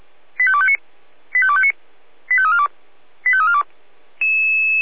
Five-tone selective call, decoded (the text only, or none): 85259 85259 85342 85342